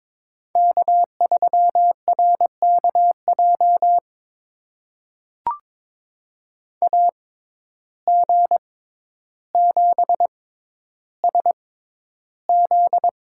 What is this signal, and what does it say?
Morse code, heard as a keyed tone, about 21 wpm, spaced as for 11 wpm, 700 Hz. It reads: K3RKJ AG7SZ
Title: USA Call Signs - 5 Characters - 6x Character Spacing - 22wpm